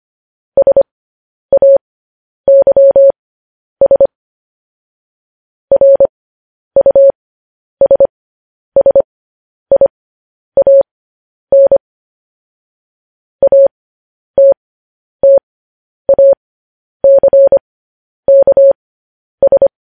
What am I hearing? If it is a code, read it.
Morse code, heard as a keyed tone, about 25 wpm, spaced as for 10 wpm, 550 Hz. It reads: SAYS RUSSIAN ATTACKS
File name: podcast / News headlines-1x-25x10